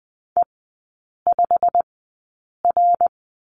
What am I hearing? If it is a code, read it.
Morse code, heard as a keyed tone, about 20 wpm, spaced as for 8 wpm, 700 Hz. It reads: E5R
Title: 087 Lesson 15e - Number 5 - Three Characters 20wpm